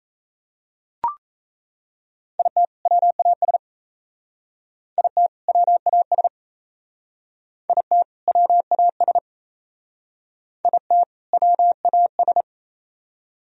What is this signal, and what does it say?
Morse code, heard as a keyed tone, about 34 wpm, 700 Hz. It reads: IT WAS IT WAS IT WAS IT WAS
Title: Speed-Racing - Sets of 2 Words 28wpm